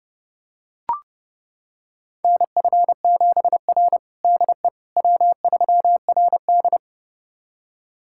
Morse code, heard as a keyed tone, about 32 wpm, 700 Hz. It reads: NF7R DE W3RD
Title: USA Callsign DE Callsign - 4 Characters - 30wpm